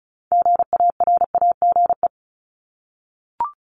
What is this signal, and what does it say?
Morse code, heard as a keyed tone, about 36 wpm, 700 Hz. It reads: GARAGE
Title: Multi-letter Phonemes 35wpm